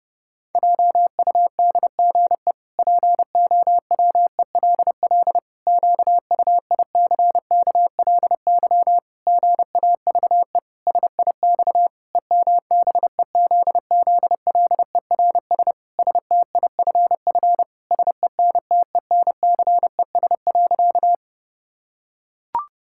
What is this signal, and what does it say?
Morse code, heard as a keyed tone, about 28 wpm, 700 Hz. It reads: JUDGE POWELL QUICKLY GAVE SIX EMBEZZLERS STIFF SENTENCES.
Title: Pangram Challenge v2 - 30wpm